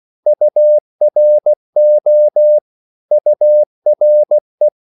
Morse code, heard as a keyed tone, about 16 wpm, 600 Hz. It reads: URO URE
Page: Korte danske ord 16wpm